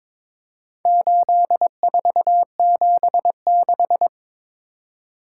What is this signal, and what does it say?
Morse code, heard as a keyed tone, about 22 wpm, 700 Hz. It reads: 8476
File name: practice / 22wpm / Four Letter-Number Combinations - 22wpm